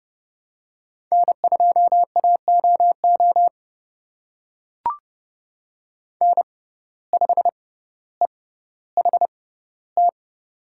Morse code, heard as a keyed tone, about 29 wpm, spaced as for 15 wpm, 700 Hz. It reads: N2AOO N5EHT
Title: USA Call Signs - 5 Characters - 6x Character Spacing - 30wpm